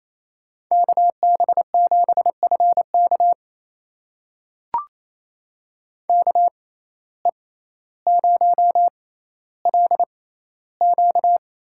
Morse code, heard as a keyed tone, about 28 wpm, spaced as for 14 wpm, 700 Hz. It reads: KB7FK KE0LQ
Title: USA Call Signs - 5 Characters - 6x Character Spacing - 28wpm